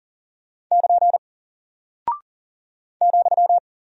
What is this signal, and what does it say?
Morse code, heard as a keyed tone, about 40 wpm, 700 Hz. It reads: <KN> ,
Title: Single Character (Letter:Number:Punctuation:Prosign) Alt 40wpm